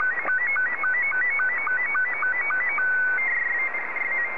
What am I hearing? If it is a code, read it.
SSTV (slow-scan television), Scottie 2.
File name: sstv / S2-JA3QJQ